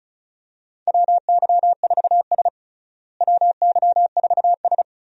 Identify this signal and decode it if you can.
Morse code, heard as a keyed tone, about 36 wpm, 700 Hz. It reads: WY4S WY4S
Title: USA Call Signs - 4 Characters - Repeated 2x - 35wpm